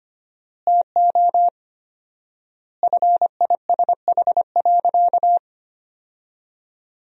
Morse code, heard as a keyed tone, about 25 wpm, 700 Hz. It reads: TO FISH.
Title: 4x Word-Spacing - Sentences from Top 300 Words - 25wpm